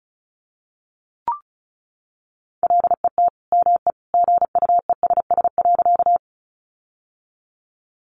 Morse code, heard as a keyed tone, about 34 wpm, 700 Hz. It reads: LET ME GUESS.
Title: Sentences from Top 1500 Words 35wpm